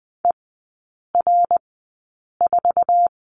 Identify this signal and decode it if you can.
Morse code, heard as a keyed tone, about 20 wpm, spaced as for 8 wpm, 700 Hz. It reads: ER4
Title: 060 Lesson 11e - Letter h - Three Characters 20wpm